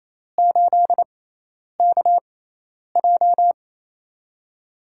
Morse code, heard as a keyed tone, about 28 wpm, spaced as for 10 wpm, 700 Hz. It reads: 8KJ
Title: USA Call Signs - 5 Characters - 6x Character Spacing - 28wpm